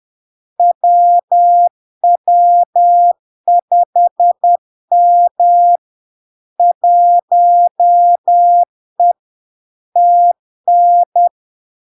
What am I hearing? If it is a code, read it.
Morse code, heard as a keyed tone, about 10 wpm, 700 Hz. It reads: WW5M 1E TN
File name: LARK-FD-2022-Log-in-CW-10wpm-700-Hz